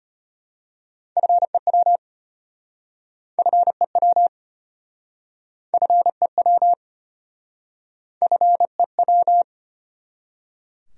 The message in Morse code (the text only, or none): FEW FEW FEW FEW